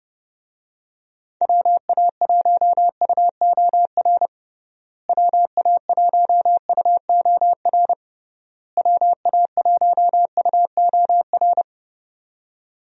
Morse code, heard as a keyed tone, about 30 wpm, 700 Hz. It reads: WA1UOR WA1UOR WA1UOR
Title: USA Call Signs - 6 Characters - Repeated 3x - 30wpm